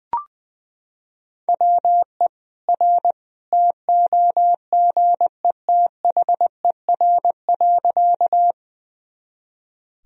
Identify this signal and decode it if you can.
Morse code, heard as a keyed tone, about 20 wpm, 700 Hz. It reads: WE R TOGETHER.
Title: Sentences from Top 300 Words and Abbreviations 20wpm